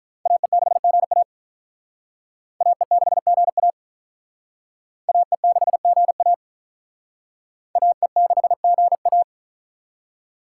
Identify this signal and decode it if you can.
Morse code, heard as a keyed tone, about 44 wpm, 700 Hz. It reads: AE6GA AE6GA AE6GA AE6GA